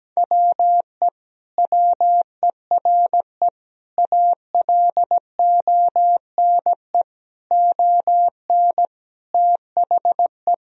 Morse code, heard as a keyed tone, about 17 wpm, 700 Hz. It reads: WE WERE ALONE ON THE
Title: Sentences from Top 1000 Words - Review Part 3 - 17wpm